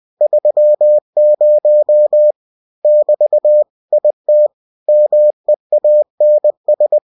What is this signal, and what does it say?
Morse code, heard as a keyed tone, about 20 wpm, 600 Hz. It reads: 30 <BT> IT MEANS
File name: cw-books / winnie-the-pooh / winnie-20wpm-page-0030